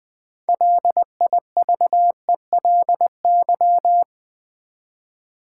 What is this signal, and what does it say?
Morse code, heard as a keyed tone, about 20 wpm, 700 Hz. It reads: LIVELY
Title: Build A Word - 3 Letters At A Time - 6-Letter Words - 20wpm